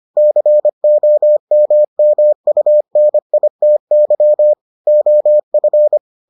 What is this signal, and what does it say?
Morse code, heard as a keyed tone, about 25 wpm, 600 Hz. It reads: COMMUNITY OF